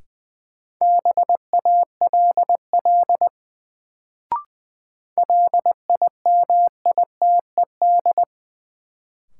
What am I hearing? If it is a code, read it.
Morse code, heard as a keyed tone, about 20 wpm, 700 Hz. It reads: BALL LIMITED
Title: Sentences from Top 900 Words - New Word Drill - 20wpm